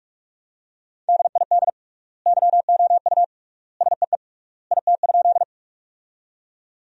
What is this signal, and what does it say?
Morse code, heard as a keyed tone, about 44 wpm, 700 Hz. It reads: DID YOU SEE IT?
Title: 3x Word-Spacing - Sentences from Top 100 Words - 45wpm